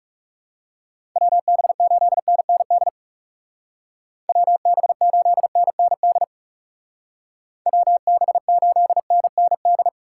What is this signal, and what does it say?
Morse code, heard as a keyed tone, about 35 wpm, 700 Hz. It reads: WB8NND WB8NND WB8NND
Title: Speed Racing - USA Call Signs 30wpm